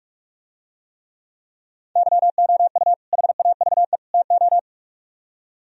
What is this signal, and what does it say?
Morse code, heard as a keyed tone, about 45 wpm, 700 Hz. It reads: YOU HAVE TO